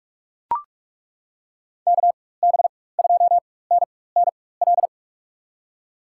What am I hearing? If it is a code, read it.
Morse code, heard as a keyed tone, about 42 wpm, spaced as for 20 wpm, 700 Hz. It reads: KB2NNL
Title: USA Call Signs - 6 Characters - 4x Character Spacing - 45wpm